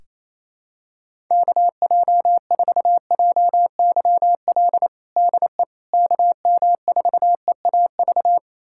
Morse code, heard as a keyed tone, about 28 wpm, 700 Hz. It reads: KJ4JYL DE KM4EAV
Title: USA Callsign DE Callsign 28wpm